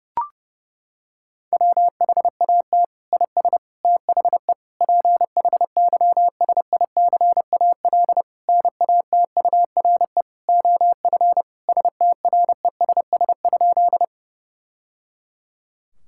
Morse code, heard as a keyed tone, about 30 wpm, 700 Hz. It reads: WHAT IS THE PHYSICAL NATURE OF STRESS?
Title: Sentences from Top 1000 Words - Review Part 6 - 30wpm